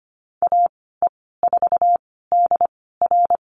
Morse code, heard as a keyed tone, about 25 wpm, spaced as for 16 wpm, 700 Hz. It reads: AE4DR
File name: Lesson 15g - Number 5 - Call Signs 25wpm